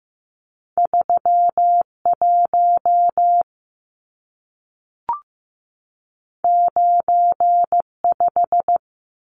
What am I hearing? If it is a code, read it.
Morse code, heard as a keyed tone, about 15 wpm, 700 Hz. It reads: 3 1 9 5
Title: Numbers - 2-Digit 15wpm